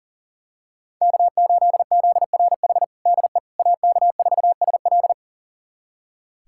Morse code, heard as a keyed tone, about 40 wpm, 700 Hz. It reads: K8ZRH DE AK4SL
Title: USA Callsign DE Callsign - 5 Characters - 40wpm